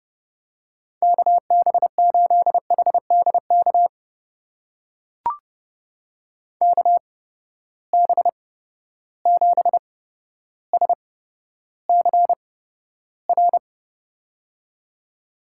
Morse code, heard as a keyed tone, about 30 wpm, spaced as for 13 wpm, 700 Hz. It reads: KB8HDK KB7SCR
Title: USA Call Signs - 8x Character Spacing - 30wpm